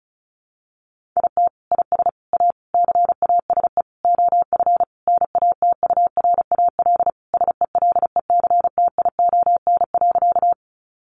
Morse code, heard as a keyed tone, about 34 wpm, 700 Hz. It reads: IT IS A CASE OF NATURAL SELECTION.
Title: Sentences from Top 1400 Words 35wpm